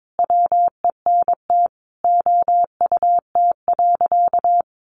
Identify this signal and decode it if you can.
Morse code, heard as a keyed tone, about 22 wpm, 700 Hz. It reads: WENT OUT.
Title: Sentences from Top 600 Words 22wpm